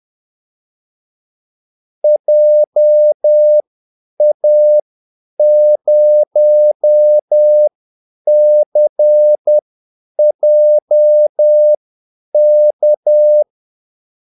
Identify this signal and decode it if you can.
Morse code, heard as a keyed tone, about 10 wpm, spaced as for 8 wpm, 600 Hz. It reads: JA0CJK
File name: Callsigns-10-8wpm